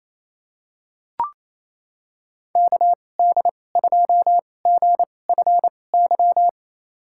Morse code, heard as a keyed tone, about 27 wpm, spaced as for 20 wpm, 700 Hz. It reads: KD2GFY
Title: USA Call Signs - 2x Character Spacing - 28wpm